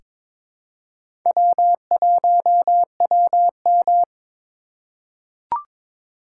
Morse code, heard as a keyed tone, about 22 wpm, 700 Hz. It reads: W1WM
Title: USA Call Signs - 4 Characters - 3x Character Spacing - 22wpm